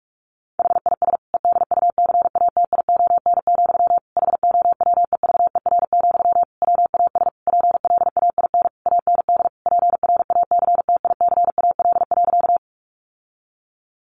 Morse code, heard as a keyed tone, about 44 wpm, 700 Hz. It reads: HIS EDUCATION, HOWEVER, WAS PLAIN AND PRACTICAL.
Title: Sentences from Top 1400 Words 45wpm